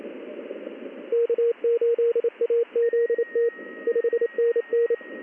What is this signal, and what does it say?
Morse code, heard as a keyed tone, about 28 wpm, 470 Hz, with radio noise. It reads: K8AZT 5NN
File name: contests / ss-ssb-k5tr- -1700